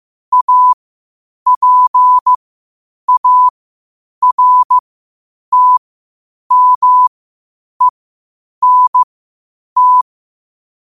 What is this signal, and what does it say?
Morse code, heard as a keyed tone, about 15 wpm, spaced as for 8 wpm, 1000 Hz. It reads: APARTMENT
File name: qotdfiles / 8wpm